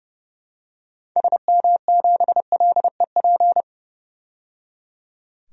Morse code, heard as a keyed tone, about 30 wpm, 700 Hz. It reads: SM7LEP
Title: DXCC-100 Call Signs - 6 Characters - 30wpm